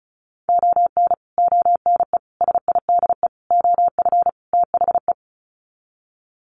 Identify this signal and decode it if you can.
Morse code, heard as a keyed tone, about 34 wpm, 700 Hz. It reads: ON ONE SIDE OF THE
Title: Sets of 5 Words 35wpm